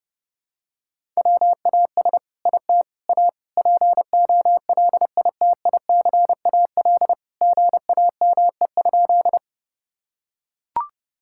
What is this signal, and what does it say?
Morse code, heard as a keyed tone, about 30 wpm, 700 Hz. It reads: WAS IT A POLITICAL GAME?
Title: Sentences from Top 1000 Words - Review Part 5 - 30wpm